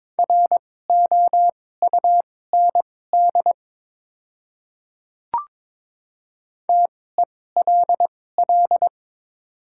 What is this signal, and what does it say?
Morse code, heard as a keyed tone, about 22 wpm, spaced as for 16 wpm, 700 Hz. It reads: ROUND TELL
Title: 2x Character Spacing - Top 200 Words - 22wpm